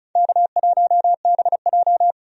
Morse code, heard as a keyed tone, about 34 wpm, 700 Hz. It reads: K1BJ
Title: USA Call Signs - 4 Characters - 35wpm